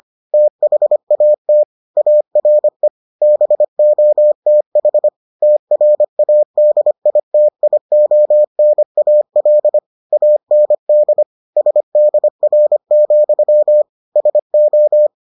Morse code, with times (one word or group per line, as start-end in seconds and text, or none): 0.3-1.6 THAT
2.0-2.9 ARE
3.2-5.1 BOTH
5.4-9.8 TRADITIONAL
10.1-11.2 AND
11.6-13.8 SDR,
14.2-15.1 SO